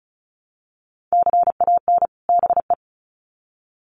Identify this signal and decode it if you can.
Morse code, heard as a keyed tone, about 34 wpm, 700 Hz. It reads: CAN BE
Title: Sets of 2 Words 35wpm